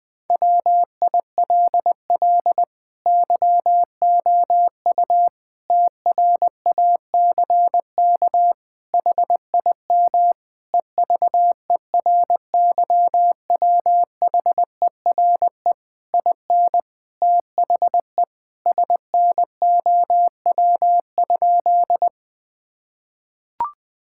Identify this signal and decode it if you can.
Morse code, heard as a keyed tone, about 20 wpm, 700 Hz. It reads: WILL YOU TRACK HIM EVERYWHERE IN THE SNOW?
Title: Sentences from Top 1600 Words 20wpm